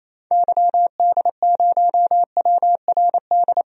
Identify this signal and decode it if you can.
Morse code, heard as a keyed tone, about 28 wpm, 700 Hz. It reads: YD0WRD
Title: DXCC-100 Call Signs - 6 Characters - 28wpm